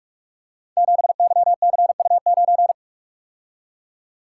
Morse code, heard as a keyed tone, about 44 wpm, 700 Hz. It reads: 7YCU9
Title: Code Groups - 5 Letter:Numbers - 45wpm